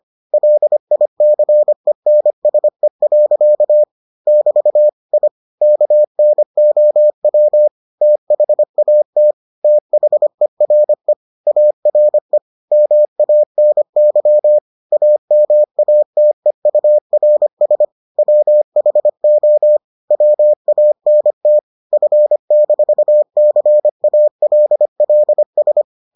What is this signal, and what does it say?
Morse code, heard as a keyed tone, about 25 wpm, 600 Hz. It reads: LICENSE. <BT> I KNOW THAT THERE ARE MANY AMATEURS WHO WANT F-CALLS